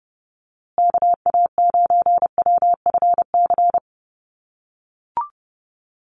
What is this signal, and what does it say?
Morse code, heard as a keyed tone, about 30 wpm, 700 Hz. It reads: KA9WFC
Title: USA Call Signs Repeated 2x 30wpm